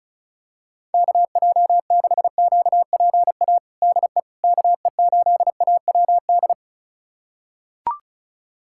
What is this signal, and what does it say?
Morse code, heard as a keyed tone, about 36 wpm, 690 Hz. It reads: KJ6QPA DE KE8AWD
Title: USA Callsign DE Callsign - 6 Characters - 35wpm